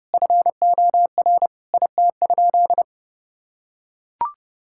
Morse code, heard as a keyed tone, about 30 wpm, 700 Hz. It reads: FOR IT?
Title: Sentences from Top 1000 Words - Review Part 6 - 30wpm